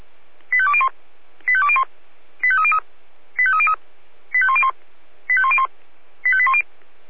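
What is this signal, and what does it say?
Five-tone selective call (ZVEI), heard as quick runs of five tones, tones of about 70 ms. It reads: 85201 85201 85303 85303 85101 85101 85810
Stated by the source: ZVEI five-tone selective call received on an SDR.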